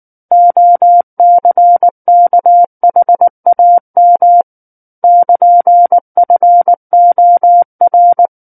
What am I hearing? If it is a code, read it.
Morse code, heard as a keyed tone, about 19 wpm, 700 Hz. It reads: OCKHAM <KN>FOR
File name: Book One / 10-Chapter 10 - 19 WPM